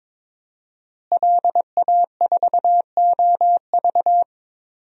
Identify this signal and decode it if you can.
Morse code, heard as a keyed tone, about 22 wpm, 700 Hz. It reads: LA4OV
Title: DXCC-100 Call Signs - 5 Characters - Repeated 3x - 22wpm